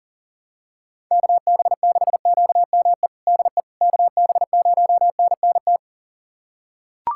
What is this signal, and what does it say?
Morse code, heard as a keyed tone, about 40 wpm, 700 Hz. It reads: KB6QME DE KB0NNT
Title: USA Callsign DE Callsign - 6 Characters - 40wpm